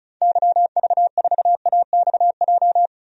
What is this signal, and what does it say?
Morse code, heard as a keyed tone, about 34 wpm, 700 Hz. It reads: YV4AXJ
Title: DXCC-100 Call Signs - 6 Characters - 35wpm